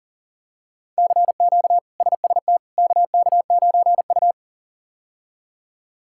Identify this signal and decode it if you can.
Morse code, heard as a keyed tone, about 40 wpm, 700 Hz. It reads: CQ SST KK9U
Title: K1USN SST QSO Practice - 40wpm